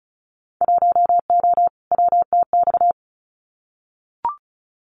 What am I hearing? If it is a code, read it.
Morse code, heard as a keyed tone, about 34 wpm, 700 Hz. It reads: 1O WTX
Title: Winter Field Day Exchanges Encore 35wpm